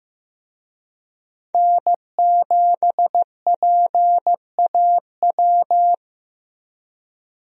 Morse code, heard as a keyed tone, about 15 wpm, 700 Hz. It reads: N7PAW
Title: USA Call Signs - 5 Characters - Repeated 3x - 15wpm